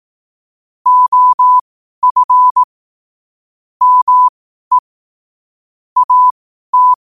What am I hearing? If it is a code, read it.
Morse code, heard as a keyed tone, about 18 wpm, spaced as for 12 wpm, 1000 Hz. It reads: OF ME AT
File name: qotdfiles / 12wpm